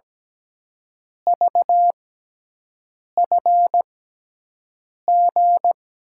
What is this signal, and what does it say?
Morse code, heard as a keyed tone, about 17 wpm, spaced as for 6 wpm, 700 Hz. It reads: VFG
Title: USA Call Signs - 6 Characters - 6x Character Spacing - 17wpm